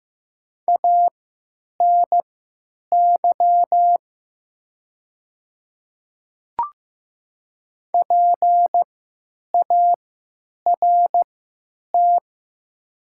Morse code, heard as a keyed tone, about 15 wpm, spaced as for 9 wpm, 700 Hz. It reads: ANY PART
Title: 3x Character Spacing - Top 200 Words - 15wpm